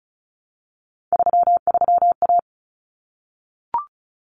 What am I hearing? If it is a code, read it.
Morse code, heard as a keyed tone, about 34 wpm, 700 Hz. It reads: 33A
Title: Common Cut Numbers - 3-Digit 35wpm